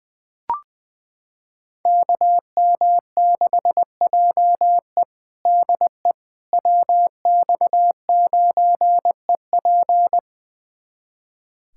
Morse code, heard as a keyed tone, about 21 wpm, 700 Hz. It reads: KM6JE DE WX9EP
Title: USA Callsign DE Callsign 20wpm